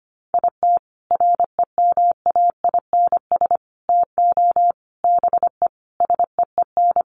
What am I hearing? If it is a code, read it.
Morse code, heard as a keyed tone, about 25 wpm, 700 Hz. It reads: IT REMAINS TO BE SEEN
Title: Sets of 5 Words Encore 25wpm